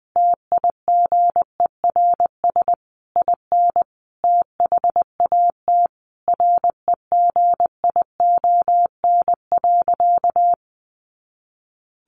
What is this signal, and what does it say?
Morse code, heard as a keyed tone, about 19 wpm, 700 Hz. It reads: TIGERS IN THAT REGION.
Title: A Sound Of Thunder 20wpm